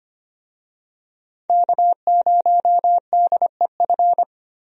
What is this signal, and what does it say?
Morse code, heard as a keyed tone, about 25 wpm, 700 Hz. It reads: K0DEF
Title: USA Call Signs - 5 Characters - Repeated 2x - 25wpm